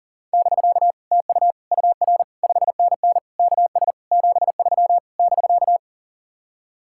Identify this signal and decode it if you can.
Morse code, heard as a keyed tone, about 48 wpm, 700 Hz. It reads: <BK> TU UR 5NN KS 73 <BK>
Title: Speed-Racing Common QSO Phrases - 40wpm